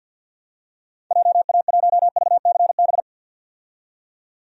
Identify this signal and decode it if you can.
Morse code, heard as a keyed tone, about 51 wpm, 700 Hz. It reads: JA1VCB
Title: DXCC-100 Call Signs - 6 Characters - 50wpm